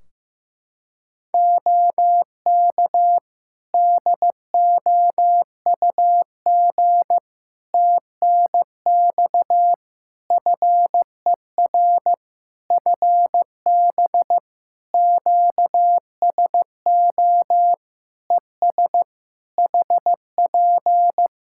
Morse code, heard as a keyed tone, about 15 wpm, 700 Hz. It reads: OK DOUG TNX FER FB QSO ES HP
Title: K1USN SST QSO Practice 15wpm